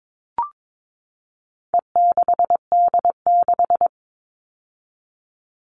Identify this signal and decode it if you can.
Morse code, heard as a keyed tone, about 22 wpm, 700 Hz. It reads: E6D6
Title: All Cut Numbers - 4-Digit 22wpm